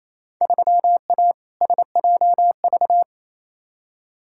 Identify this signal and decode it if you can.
Morse code, heard as a keyed tone, about 28 wpm, 700 Hz. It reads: 3A SJV
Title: Field Day Exchanges Encore - 28wpm